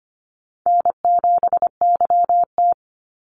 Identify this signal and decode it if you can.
Morse code, heard as a keyed tone, about 25 wpm, 700 Hz. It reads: N7YT
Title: Call Signs Repeated 3x - 25wpm